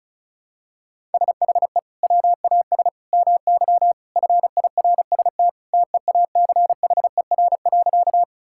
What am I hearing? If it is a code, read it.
Morse code, heard as a keyed tone, about 36 wpm, 700 Hz. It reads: SHE WAS MY FIRST TEACHER.
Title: Sentences from Top 1000 Words - Review Part 6 - 35wpm